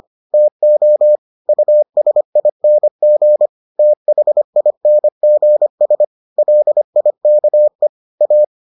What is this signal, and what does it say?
Morse code, heard as a keyed tone, about 25 wpm, 600 Hz. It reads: TO USING THINGS LIKE A